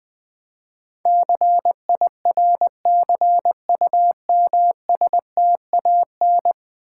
Morse code, heard as a keyed tone, about 20 wpm, 700 Hz. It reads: CIRCUMSTAN
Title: Build A Word - 5 Letters At A Time - 12-Letter Words - 20wpm